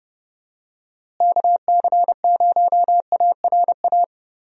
Morse code, heard as a keyed tone, about 30 wpm, 700 Hz. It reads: KC0ARA
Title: USA Call Signs - 6 Characters - Repeated 2x - 30wpm